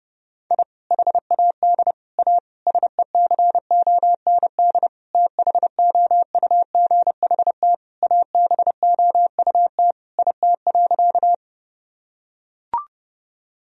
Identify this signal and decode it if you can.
Morse code, heard as a keyed tone, about 29 wpm, 700 Hz. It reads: I HAD A SECOND THOUGHT ABOUT IT.
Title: Sentences from Top 1000 Words - Review Part 1 - 30wpm